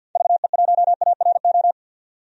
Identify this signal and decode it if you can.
Morse code, heard as a keyed tone, about 52 wpm, 700 Hz. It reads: VE1ARO
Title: DXCC-100 Call Signs - 6 Characters - 50wpm